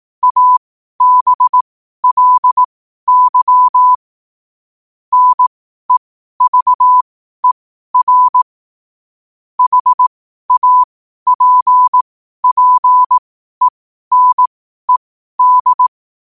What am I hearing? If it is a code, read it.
Morse code, heard as a keyed tone, about 18 wpm, spaced as for 12 wpm, 1000 Hz. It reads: ABLY NEVER HAPPENED
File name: qotdfiles / 12wpm